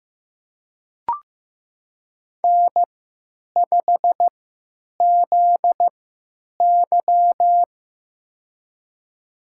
Morse code, heard as a keyed tone, about 15 wpm, spaced as for 9 wpm, 700 Hz. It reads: N5ZY
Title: USA Call Signs - 4 Characters - 3x Character Spacing - 15wpm